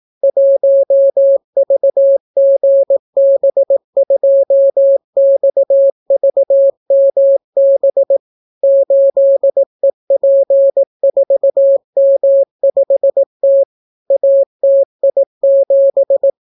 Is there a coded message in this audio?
Yes: Morse code from 0.2 to 16.3 s, heard as a keyed tone, about 18 wpm, 550 Hz. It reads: 1VGB2XVMB 8EP4M5T ATI7